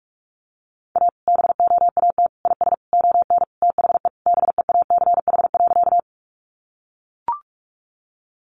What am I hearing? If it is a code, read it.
Morse code, heard as a keyed tone, about 44 wpm, 700 Hz. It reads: A BOAT IS ON THE BEACH.
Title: Sentences from Top 1500 Words 45wpm